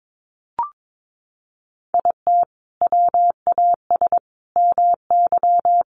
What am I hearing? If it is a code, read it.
Morse code, heard as a keyed tone, about 22 wpm, 700 Hz. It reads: IT WAS MY
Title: Sentences from Top 1200 Words 22wpm